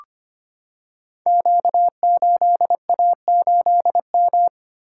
Morse code, heard as a keyed tone, about 25 wpm, 700 Hz. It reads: Q8A8M
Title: Code Groups - 5 Letter:Numbers - 25wpm